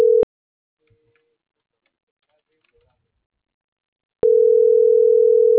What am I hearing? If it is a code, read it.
Ringback tone (North America), 440 + 480 Hz.